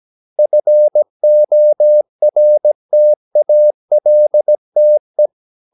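Morse code, heard as a keyed tone, about 17 wpm, 600 Hz. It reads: FORTALTE
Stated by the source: Slow (QRS) Morse practice at 17 wpm, from CW med Gnister.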